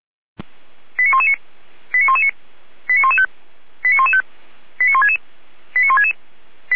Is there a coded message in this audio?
Yes: five-tone selective call (ZVEI) from 1.0 to 6.1 s, heard as quick runs of five tones, tones of about 70 ms. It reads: 89119 89119 89115 89115 89166 89166